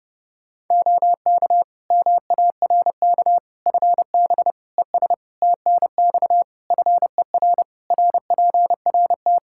Morse code, heard as a keyed tone, about 32 wpm, 700 Hz. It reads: OK MARK FB ES TNX FER RPRT
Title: Practice QSO - Exchange 2 - Info - 30wpm